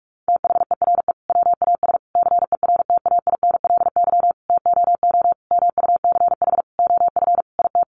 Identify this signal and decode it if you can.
Morse code, heard as a keyed tone, about 44 wpm, 700 Hz. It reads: THERE WAS CERTAINLY TOO MUCH OF IT
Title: Sentences from Top 900 Words 45wpm